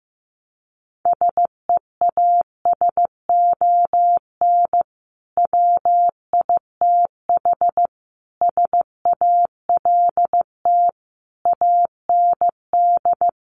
Morse code, heard as a keyed tone, about 15 wpm, 700 Hz. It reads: SEASON WITH SALT AND
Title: Sets of 5 Words 15wpm